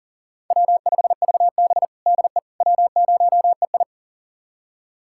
Morse code, heard as a keyed tone, about 40 wpm, 700 Hz. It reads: W5VB DE W0EI
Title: USA Callsign DE Callsign - 4 Characters - 40wpm